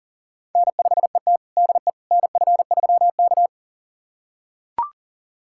Morse code, heard as a keyed tone, about 37 wpm, 700 Hz. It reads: N5ET DE NF3K
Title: USA Callsign DE Callsign - 4 Characters - 40wpm